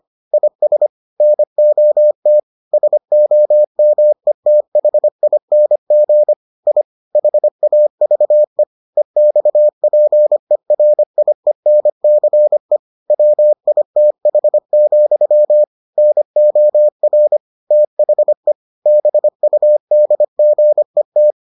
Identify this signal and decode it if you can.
Morse code, heard as a keyed tone, about 25 wpm, 600 Hz. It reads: IS NOT SOMETHING I HAVE EXPERIENCE WITH, NOR THE BUDGET